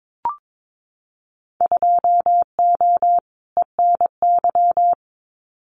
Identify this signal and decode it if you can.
Morse code, heard as a keyed tone, about 22 wpm, 700 Hz. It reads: 2O ENY
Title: Winter Field Day Exchanges 22wpm